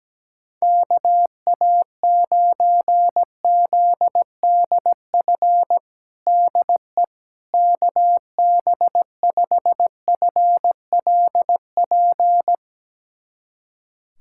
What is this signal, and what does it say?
Morse code, heard as a keyed tone, about 17 wpm, 700 Hz. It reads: KA9ZDF DE KB5FLP
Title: USA Callsign DE Callsign - 6 Characters - 17wpm